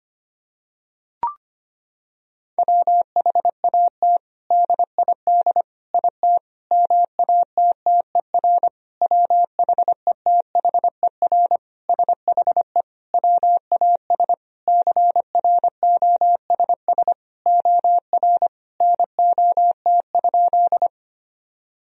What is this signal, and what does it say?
Morse code, heard as a keyed tone, about 25 wpm, 700 Hz. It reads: WHAT DID IT MATTER WHETHER SHE WAS CROSS OR NOT?
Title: Sentences from Top 1000 Words - Review Part 3 - 25wpm